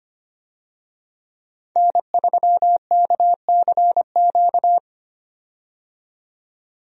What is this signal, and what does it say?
Morse code, heard as a keyed tone, about 25 wpm, 700 Hz. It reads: N3KCQ
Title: USA Call Signs - 5 Characters - 25wpm